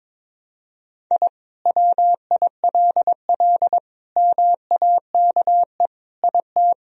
Morse code, heard as a keyed tone, about 22 wpm, 700 Hz. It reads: I WILL MAKE IT
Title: Sentences from Top 1000 Words - Review Part 2 - 22wpm